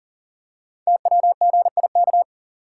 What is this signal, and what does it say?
Morse code, heard as a keyed tone, about 41 wpm, 690 Hz. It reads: TWGIK
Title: Code Groups - 5 Letters - 40wpm